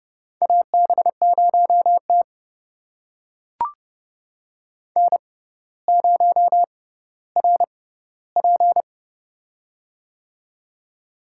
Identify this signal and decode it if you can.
Morse code, heard as a keyed tone, about 30 wpm, spaced as for 15 wpm, 700 Hz. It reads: AB0T N0RP
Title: USA Call Signs - 4 Characters - 6x Character Spacing - 30wpm